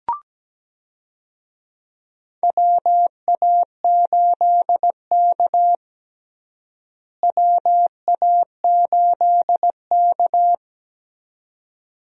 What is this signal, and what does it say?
Morse code, heard as a keyed tone, about 17 wpm, 700 Hz. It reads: WA8K WA8K
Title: USA Call Signs - 4 Characters - Repeated 2x - 17wpm